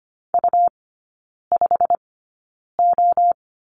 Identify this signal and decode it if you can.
Morse code, heard as a keyed tone, about 25 wpm, spaced as for 9 wpm, 700 Hz. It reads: U5O
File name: course / 25:10wpm / 100 Lesson 17e - Letter u - Three Characters 25wpm